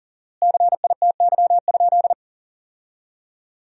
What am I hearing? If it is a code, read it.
Morse code, heard as a keyed tone, about 40 wpm, 700 Hz. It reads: CITY?
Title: 4x Word-Spacing - Sentences from Top 300 Words - 40wpm